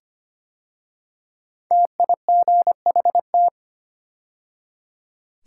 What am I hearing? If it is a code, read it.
Morse code, heard as a keyed tone, about 25 wpm, 700 Hz. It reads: TIGHT
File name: 5 Letter Words - First 3 Letters Given - 25wpm